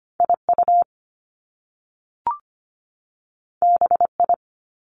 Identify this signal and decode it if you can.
Morse code, heard as a keyed tone, about 25 wpm, 700 Hz. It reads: IU BI
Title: Top Two Letter Combinations in Words 25wpm